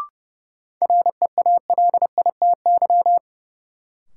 Morse code, heard as a keyed tone, about 30 wpm, 700 Hz. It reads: REALITY
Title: Sentences from Top 1200 Words - New Word Drill - 30wpm